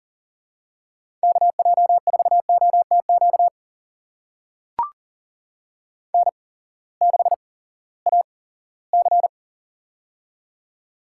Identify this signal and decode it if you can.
Morse code, heard as a keyed tone, about 40 wpm, spaced as for 20 wpm, 700 Hz. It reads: KJ4OTQ N6AC
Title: USA Call Signs - 8x Character Spacing - 40wpm